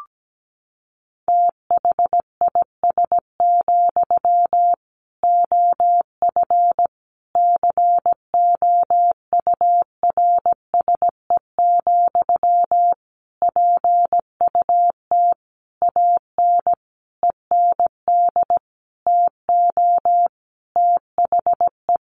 Morse code, heard as a keyed tone, about 17 wpm, 700 Hz. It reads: THIS, OF COURSE, PUT AN END TO THE
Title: Sentences from Top 1400 Words 17wpm